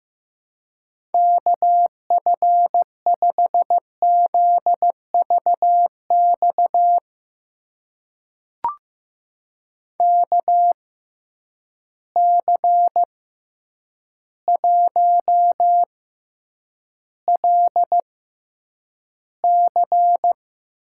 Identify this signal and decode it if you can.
Morse code, heard as a keyed tone, about 15 wpm, spaced as for 8 wpm, 700 Hz. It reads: KF5ZVX KC1LC
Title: USA Call Signs - 6 Characters - 6x Character Spacing - 15wpm